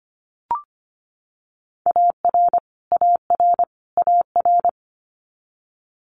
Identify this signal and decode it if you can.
Morse code, heard as a keyed tone, about 25 wpm, 700 Hz. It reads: AR AR AR
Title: Multi-letter Phonemes Repeated 3x 25wpm